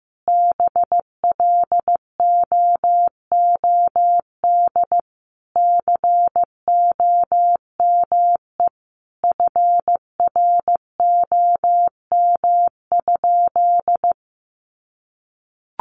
Morse code, heard as a keyed tone, about 15 wpm, 700 Hz. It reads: BLOOD COME FROM?
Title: Sentences from Top 1100 Words 15wpm